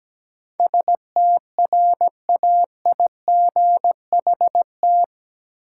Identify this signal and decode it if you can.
Morse code, heard as a keyed tone, about 17 wpm, 700 Hz. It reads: STRAIGHT
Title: Sentences from Top 1000 Words - New Word Drill - 17wpm